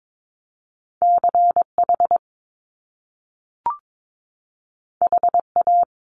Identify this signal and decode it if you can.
Morse code, heard as a keyed tone, about 22 wpm, 700 Hz. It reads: CH HA
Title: Top Two Letter Combinations in Words 22wpm